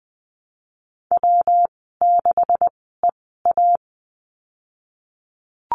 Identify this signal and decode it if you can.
Morse code, heard as a keyed tone, about 20 wpm, spaced as for 15 wpm, 700 Hz. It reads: W6EA
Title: 147 Lesson 24g - Letter y - Call Signs 20wpm